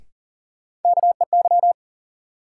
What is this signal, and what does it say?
Morse code, heard as a keyed tone, about 40 wpm, 700 Hz. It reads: KEY
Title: Sentences from Top 600 Words - New Word Drill - 40wpm